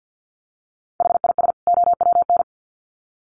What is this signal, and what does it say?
Morse code, heard as a keyed tone, about 50 wpm, 700 Hz. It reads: HIS OWN
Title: Sets of 2 Words 50wpm